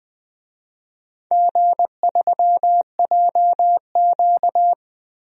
Morse code, heard as a keyed tone, about 20 wpm, 700 Hz. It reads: G3JQ
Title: DXCC-100 Call Signs - 4 Characters - Repeated 3x - 20wpm